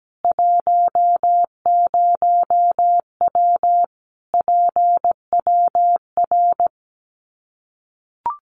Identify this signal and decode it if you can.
Morse code, heard as a keyed tone, about 17 wpm, 700 Hz. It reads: 10W PWR
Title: Common QSO Phrases 17wpm